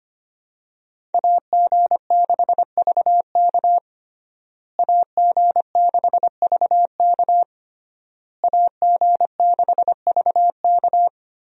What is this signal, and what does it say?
Morse code, heard as a keyed tone, about 25 wpm, 700 Hz. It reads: AG6VK AG6VK AG6VK
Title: USA Call Signs - 5 Characters - Repeated 3x - 25wpm